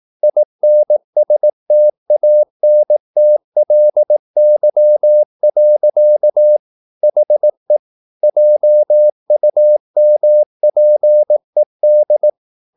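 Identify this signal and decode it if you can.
Morse code, heard as a keyed tone, about 18 wpm, 600 Hz. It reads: INSTANTLY. HE JUMPED
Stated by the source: Morse practice at 18 wpm, English text, from CW med Gnister.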